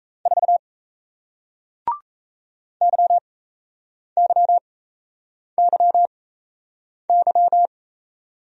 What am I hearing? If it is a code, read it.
Morse code, heard as a keyed tone, about 37 wpm, spaced as for 8 wpm, 700 Hz. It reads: 4 YYYY